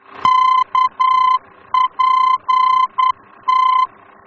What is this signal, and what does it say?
Morse code, heard as a keyed tone, about 9 wpm, 1030 Hz. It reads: KPT
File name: germany / mx / KPT-D-VOR